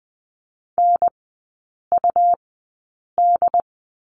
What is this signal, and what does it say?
Morse code, heard as a keyed tone, about 20 wpm, spaced as for 8 wpm, 700 Hz. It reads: NUD
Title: 107 Lesson 18e - Letter m - Three Characters 20wpm